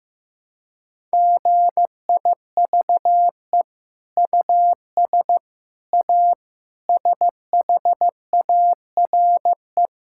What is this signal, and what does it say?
Morse code, heard as a keyed tone, about 16 wpm, 700 Hz. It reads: GIVE US A SHARE
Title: Sentences from Top 1000 Words - Review Part 6 - 15wpm